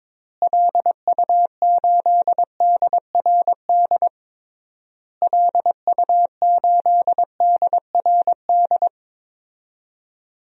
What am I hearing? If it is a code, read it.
Morse code, heard as a keyed tone, about 22 wpm, 700 Hz. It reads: LU8DRD LU8DRD
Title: DXCC-100 Call Signs - 6 Characters - Repeated 3x - 22wpm